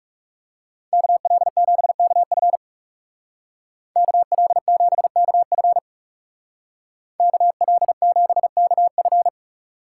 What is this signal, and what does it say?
Morse code, heard as a keyed tone, about 36 wpm, 700 Hz. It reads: KL7KF KL7KF KL7KF
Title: Speed Racing - USA 2x2 Call Signs 30wpm